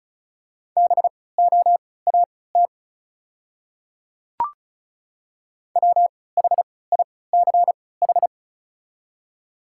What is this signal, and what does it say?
Morse code, heard as a keyed tone, about 36 wpm, spaced as for 20 wpm, 690 Hz. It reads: BOAT WHICH
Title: 3x Character Spacing - Top 500 Words - 35wpm